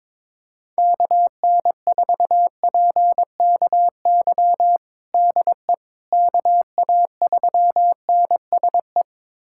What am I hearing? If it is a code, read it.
Morse code, heard as a keyed tone, about 22 wpm, 700 Hz. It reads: KN4PKY DE KA3NSE
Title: USA Callsign DE Callsign - 6 Characters - 22wpm